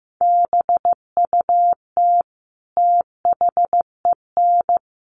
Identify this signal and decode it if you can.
Morse code, heard as a keyed tone, about 15 wpm, 700 Hz. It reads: BUT THEN
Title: Sentences from Top 300 Words 15wpm